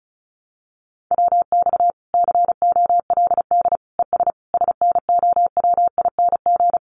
Morse code, heard as a keyed tone, about 34 wpm, 700 Hz. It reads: WX COLD ES SNOWING
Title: Common QSO Phrases 35wpm